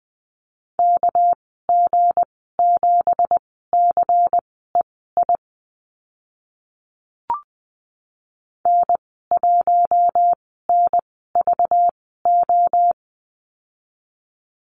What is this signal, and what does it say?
Morse code, heard as a keyed tone, about 20 wpm, spaced as for 14 wpm, 700 Hz. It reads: KG7CEI N1NVO
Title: 238 Lesson 38g - Letter q - Call Signs 20wpm